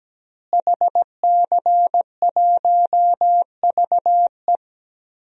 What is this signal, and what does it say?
Morse code, heard as a keyed tone, about 17 wpm, 700 Hz. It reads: HC1VE
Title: DXCC-100 Call Signs - 5 Characters - Repeated 2x - 17wpm